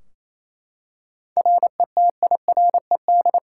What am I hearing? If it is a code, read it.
Morse code, heard as a keyed tone, about 28 wpm, 700 Hz. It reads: RETIRED
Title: Practice QSO - Exchange 3 - More Info - 28wpm